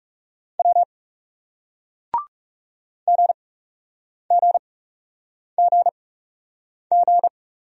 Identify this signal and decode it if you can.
Morse code, heard as a keyed tone, about 40 wpm, spaced as for 8 wpm, 700 Hz. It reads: W GGGG